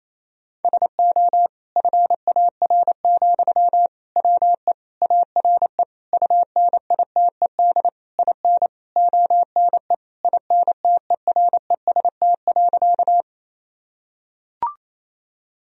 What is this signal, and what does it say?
Morse code, heard as a keyed tone, about 28 wpm, 700 Hz. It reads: SO FAR, WE ARE UNITED IN ONE INTEREST.
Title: Sentences from Top 1000 Words - Review Part 4 - 28wpm